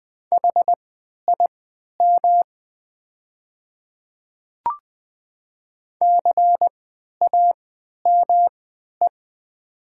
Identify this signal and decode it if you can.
Morse code, heard as a keyed tone, about 20 wpm, spaced as for 11 wpm, 700 Hz. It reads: HIM CAME
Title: 3x Character Spacing - Top 200 Words - 20wpm